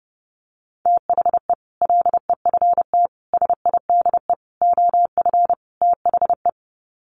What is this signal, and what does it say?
Morse code, heard as a keyed tone, about 28 wpm, 700 Hz. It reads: THE LEFT SIDE OF THE
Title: Sets of 5 Words Encore 30wpm